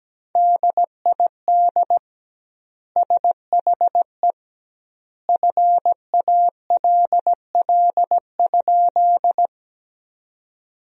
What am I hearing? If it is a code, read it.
Morse code, heard as a keyed tone, about 17 wpm, 700 Hz. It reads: DID SHE FALL?
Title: 2x Word-Spacing - Sentences from Top 500 Words - 17wpm